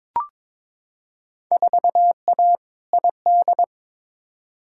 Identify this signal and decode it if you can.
Morse code, heard as a keyed tone, about 22 wpm, 700 Hz. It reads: 4A ID
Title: Field Day Exchanges Encore - 22wpm